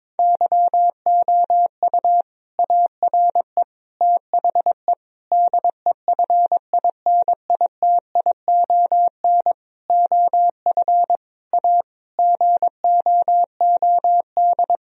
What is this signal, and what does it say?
Morse code, heard as a keyed tone, about 21 wpm, 700 Hz. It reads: YOU ARE THE DEFINITION OF A GOOD
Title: Sentences from Top 1000 Words - Review Part 4 - 22wpm